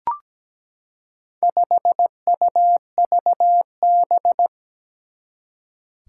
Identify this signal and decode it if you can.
Morse code, heard as a keyed tone, about 17 wpm, 700 Hz. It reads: 5UVB
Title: Four Challenging Letter-Number Combinations 17wpm